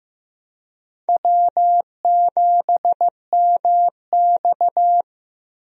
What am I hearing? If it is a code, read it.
Morse code, heard as a keyed tone, about 15 wpm, 700 Hz. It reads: W7MX
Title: USA Call Signs - 4 Characters - 15wpm